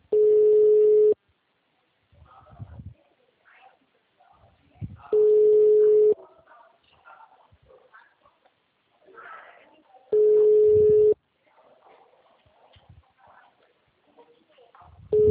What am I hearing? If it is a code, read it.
Ringback tone (Europe, 425 Hz), with speech.